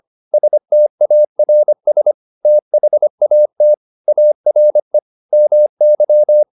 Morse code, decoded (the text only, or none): STARS THAT ARE MY